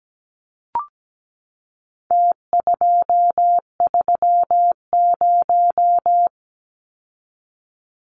Morse code, heard as a keyed tone, about 17 wpm, 700 Hz. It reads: T230
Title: Numbers - Contest Style - 4-Digit 17wpm